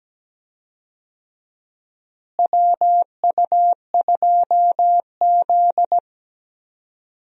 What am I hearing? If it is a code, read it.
Morse code, heard as a keyed tone, about 17 wpm, 700 Hz. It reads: WU2Z
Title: USA Call Signs - 4 Characters - Repeated 3x - 17wpm